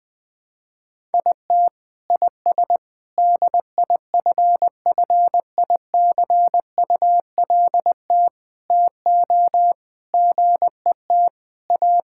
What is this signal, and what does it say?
Morse code, heard as a keyed tone, about 20 wpm, 700 Hz. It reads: IT IS DIFFICULT TO GET A
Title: Sets of 9 Words - 20wpm